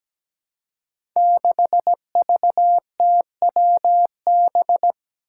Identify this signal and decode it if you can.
Morse code, heard as a keyed tone, about 17 wpm, 700 Hz. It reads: 6VTWB
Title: Five Challenging Letter-Number Combinations 17wpm